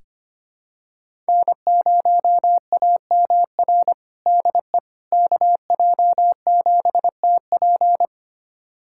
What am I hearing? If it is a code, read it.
Morse code, heard as a keyed tone, about 25 wpm, 700 Hz. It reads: N0AMR DE KJ7TP
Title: USA Callsign DE Callsign - 5 Characters - 25wpm